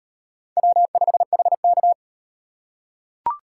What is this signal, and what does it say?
Morse code, heard as a keyed tone, about 38 wpm, 700 Hz. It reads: W5HK